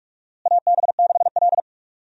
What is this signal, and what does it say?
Morse code, heard as a keyed tone, about 46 wpm, 700 Hz. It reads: AB6L
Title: USA Call Signs - 4 Characters - Repeated 3x - 45wpm